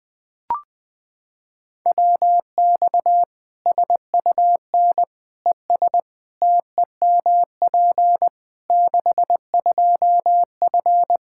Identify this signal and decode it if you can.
Morse code, heard as a keyed tone, about 20 wpm, 700 Hz. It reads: WX SUN ES TEMP 62F
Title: Practice QSO - Exchange 2 - Info - 20wpm